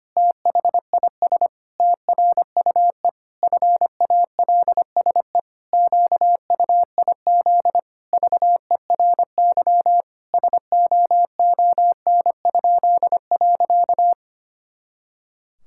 Morse code, heard as a keyed tone, about 25 wpm, 700 Hz. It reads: THIS TRUE FALSE QUIZ VERY SOON?.
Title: Pangram Challenge v2 - 25wpm